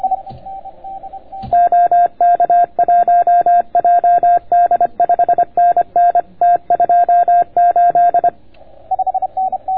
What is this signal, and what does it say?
Morse code, heard as a keyed tone, about 26 wpm, 690 Hz. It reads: OK1JD 5NN T28